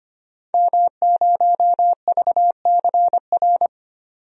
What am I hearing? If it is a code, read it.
Morse code, heard as a keyed tone, about 25 wpm, 700 Hz. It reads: M0VCR
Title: DXCC-100 Call Signs - 5 Characters - 25wpm